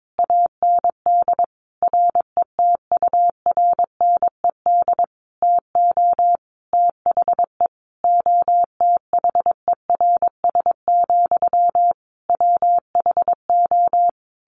Morse code, heard as a keyed tone, about 22 wpm, 700 Hz. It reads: AND RETURNED TO THE OTHERS, WHO
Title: A Sound Of Thunder 22wpm